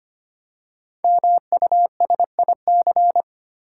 Morse code, heard as a keyed tone, about 25 wpm, 700 Hz. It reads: MUSIC
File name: practice / 25wpm / Easy Words with Hints - 25wpm